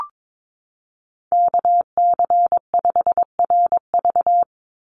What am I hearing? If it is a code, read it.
Morse code, heard as a keyed tone, about 22 wpm, 700 Hz. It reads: KC5RV
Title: USA Call Signs 22wpm